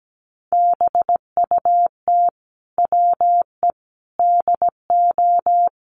Morse code, heard as a keyed tone, about 17 wpm, 700 Hz. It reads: BUT WE DO
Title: Sets of 5 Words Encore2 17wpm